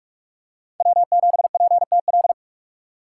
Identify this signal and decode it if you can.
Morse code, heard as a keyed tone, about 45 wpm, 700 Hz. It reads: W7PTL
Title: USA Call Signs - 5 Characters - 4x Character Spacing - 45wpm